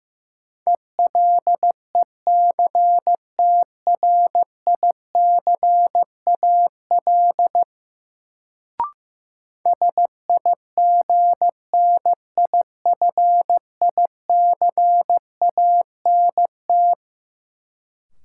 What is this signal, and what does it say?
Morse code, heard as a keyed tone, about 15 wpm, 700 Hz. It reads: ELECTRICAL SIGNIFICANT
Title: Sentences from Top 1000 Words - Review Part 6 - New Word Drill - 15wpm